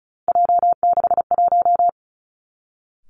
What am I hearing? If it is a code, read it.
Morse code, heard as a keyed tone, about 36 wpm, 700 Hz. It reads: J61
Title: Three Challenging Letter-Number Combinations 35wpm